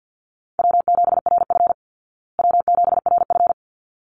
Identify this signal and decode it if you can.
Morse code, heard as a keyed tone, about 48 wpm, 700 Hz. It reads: W7RF W7RF
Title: Easy Call Signs Repeated 2x - 50wpm